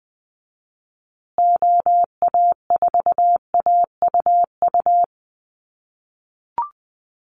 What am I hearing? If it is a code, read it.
Morse code, heard as a keyed tone, about 20 wpm, 700 Hz. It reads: OA4AUU
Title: USA Call Signs Repeated 2x 20wpm